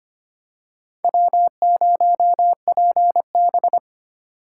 Morse code, heard as a keyed tone, about 25 wpm, 700 Hz. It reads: W0PB
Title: USA Call Signs - 4 Characters - 2x Character Spacing - 25wpm